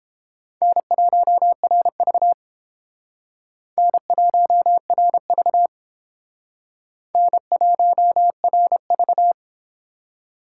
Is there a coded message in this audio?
Yes: Morse code from 0.6 to 9.3 s, heard as a keyed tone, about 30 wpm, 700 Hz. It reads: N1RV N1RV N1RV